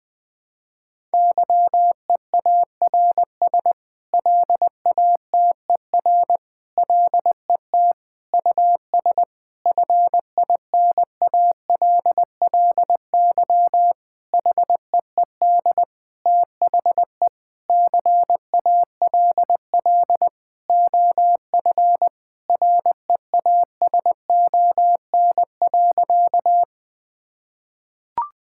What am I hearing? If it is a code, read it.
Morse code, heard as a keyed tone, about 20 wpm, 700 Hz. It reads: YEARS LATER LET US FINALLY HEED THE CALL OF REASON.
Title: Sentences with 8-Word Phrases - 20wpm